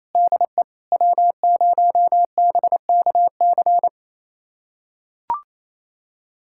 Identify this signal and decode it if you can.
Morse code, heard as a keyed tone, about 28 wpm, 700 Hz. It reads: DE W0BKC
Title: USA Callsign DE Callsign - 5 Characters - 28wpm